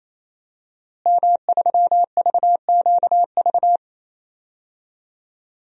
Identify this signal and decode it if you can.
Morse code, heard as a keyed tone, about 28 wpm, 700 Hz. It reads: M3VQV
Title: DXCC-100 Call Signs - 5 Characters - Repeated 3x - 28wpm